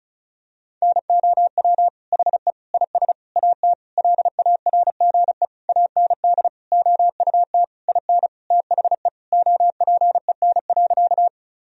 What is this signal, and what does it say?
Morse code, heard as a keyed tone, about 36 wpm, 700 Hz. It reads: NOW HE IS AT LARGE AND OUT IN THE OPEN.
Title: Sentences from Top 1000 Words - Review Part 1 - 35wpm